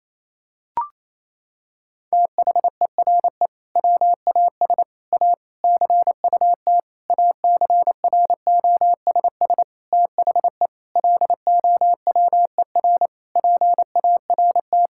Morse code, heard as a keyed tone, about 28 wpm, 700 Hz. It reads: THERE WAS A CUT ACROSS THE LOWER PART
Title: Sentences from Top 1000 Words - Review Part 5 - 28wpm